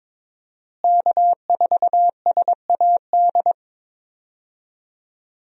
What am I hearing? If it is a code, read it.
Morse code, heard as a keyed tone, about 22 wpm, 700 Hz. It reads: K4SAD
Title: USA Call Signs - 5 Characters - Repeated 3x - 22wpm